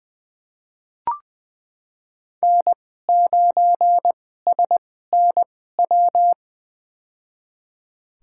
Morse code, heard as a keyed tone, about 20 wpm, spaced as for 15 wpm, 700 Hz. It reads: N9SNW
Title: USA Call Signs - 5 Characters - 2x Character Spacing - 20wpm